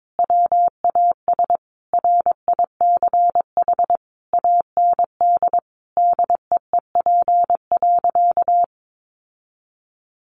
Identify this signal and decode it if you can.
Morse code, heard as a keyed tone, about 22 wpm, 700 Hz. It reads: WAS RICH AND DEEP.
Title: Sentences from Top 1100 Words 22wpm